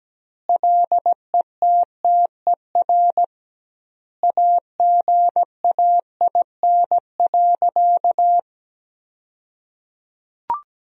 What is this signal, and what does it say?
Morse code, heard as a keyed tone, about 17 wpm, 700 Hz. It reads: LETTER AGAIN.
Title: 2x Word-Spacing - Sentences from Top 300 Words - 17wpm